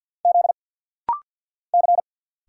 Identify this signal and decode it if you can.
Morse code, heard as a keyed tone, about 50 wpm, 690 Hz. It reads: Z C
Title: Single Letter-Number Alt - Rapid-Fire 50wpm